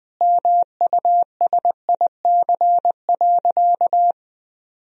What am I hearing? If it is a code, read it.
Morse code, heard as a keyed tone, about 20 wpm, 700 Hz. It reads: MUSIC.
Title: Sentences from Top 1000 Words - Review Part 1 - 20wpm